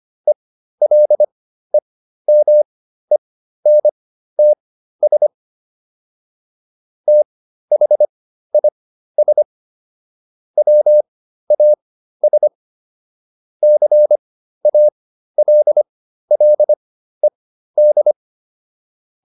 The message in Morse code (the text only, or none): ELEMENTS THIS WAS CALLED